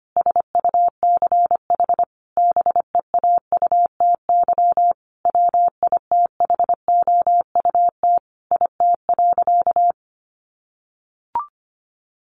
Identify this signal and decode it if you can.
Morse code, heard as a keyed tone, about 25 wpm, 700 Hz. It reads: SUCH BEAUTY WITHOUT IT.
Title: Sentences from Top 600 Words 25wpm